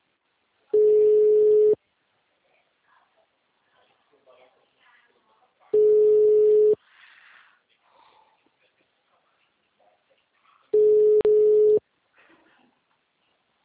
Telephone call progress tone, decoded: ringback tone

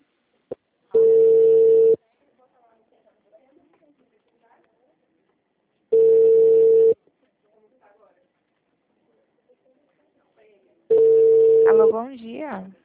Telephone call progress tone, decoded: ringback tone

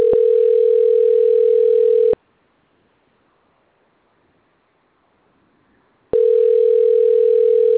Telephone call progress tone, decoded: ringback tone

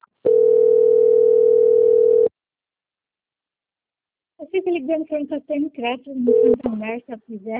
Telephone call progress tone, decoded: ringback tone

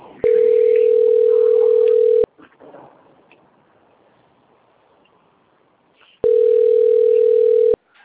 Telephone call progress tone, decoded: ringback tone

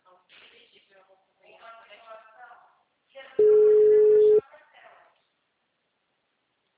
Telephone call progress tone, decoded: ringback tone

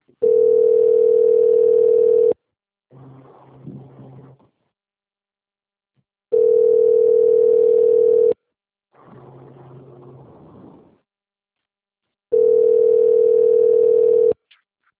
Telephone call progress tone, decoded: ringback tone

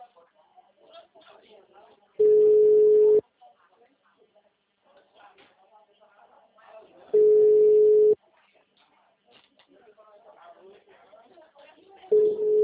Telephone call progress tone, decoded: ringback tone

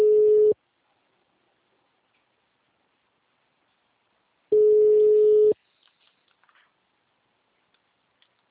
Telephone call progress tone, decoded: ringback tone

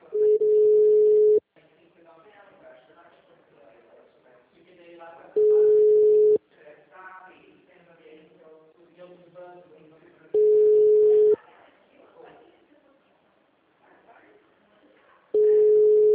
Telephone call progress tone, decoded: ringback tone

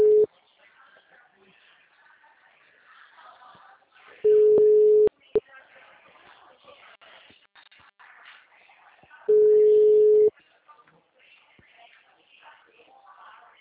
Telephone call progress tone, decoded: ringback tone

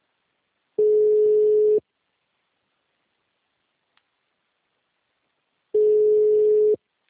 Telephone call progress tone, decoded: ringback tone

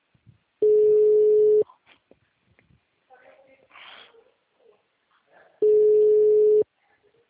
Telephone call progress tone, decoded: ringback tone

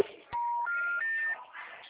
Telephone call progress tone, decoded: special information tone

